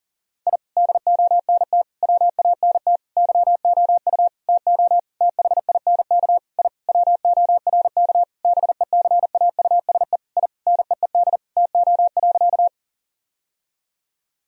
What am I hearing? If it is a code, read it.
Morse code, heard as a keyed tone, about 39 wpm, 700 Hz. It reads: I DONT WANT YOU TO THINK I WORK BECAUSE I NEED TO.
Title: Sentences from Top 1000 Words - Review Part 3 - 40wpm